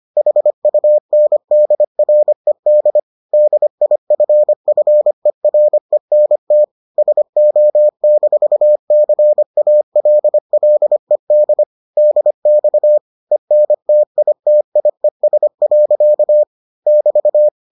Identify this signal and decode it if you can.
Morse code, heard as a keyed tone, about 25 wpm, 600 Hz. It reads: HUNDRED DIFFERENT SO-CALLED DX ENTITIES. <BT>